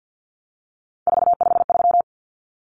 Morse code, heard as a keyed tone, about 48 wpm, 700 Hz. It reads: 453
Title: Common Cut Numbers - 3-Digit 50wpm